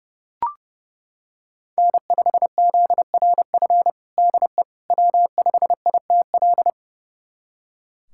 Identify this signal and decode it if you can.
Morse code, heard as a keyed tone, about 30 wpm, 700 Hz. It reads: N5ZRF DE W5ITL
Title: USA Callsign DE Callsign - 5 Characters - 30wpm